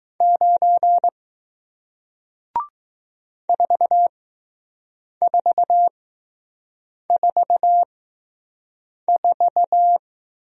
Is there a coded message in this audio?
Yes: Morse code from 0.2 to 10.0 s, heard as a keyed tone, about 20 wpm, spaced as for 6 wpm, 700 Hz. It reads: 9 4444